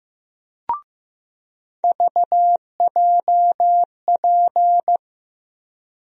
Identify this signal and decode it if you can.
Morse code, heard as a keyed tone, about 15 wpm, 700 Hz. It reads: VJP
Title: USA Call Sign Postfixes - 3 Letters 15wpm